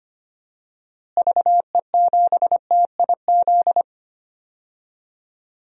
Morse code, heard as a keyed tone, about 25 wpm, 700 Hz. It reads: VE7TIZ
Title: DXCC-100 Call Signs - 6 Characters - Repeated 2x - 25wpm